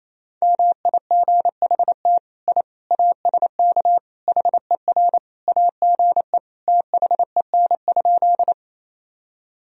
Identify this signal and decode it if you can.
Morse code, heard as a keyed tone, about 28 wpm, 700 Hz. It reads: MIGHT I ASK HER AGE THEN?
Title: Sentences from Top 1000 Words - Review Part 3 - 28wpm